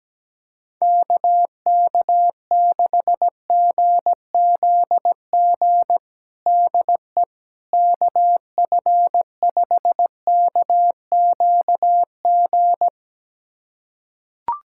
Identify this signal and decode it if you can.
Morse code, heard as a keyed tone, about 17 wpm, 700 Hz. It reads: KK6GZG DE KF5KQG
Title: USA Callsign DE Callsign - 6 Characters - 17wpm